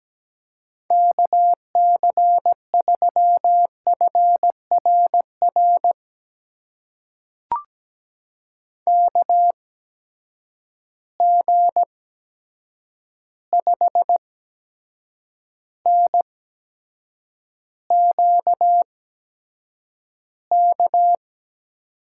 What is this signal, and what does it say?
Morse code, heard as a keyed tone, about 17 wpm, spaced as for 7 wpm, 700 Hz. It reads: KC3FRR KG5NQK
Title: USA Call Signs - 6 Characters - 8x Character Spacing - 17wpm